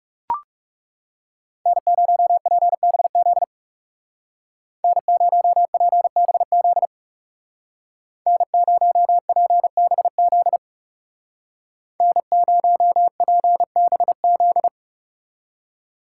Morse code, heard as a keyed tone, about 36 wpm, 700 Hz. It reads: N0PBZ N0PBZ N0PBZ N0PBZ
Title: Speed Racing - USA 1x3 Call Signs 30wpm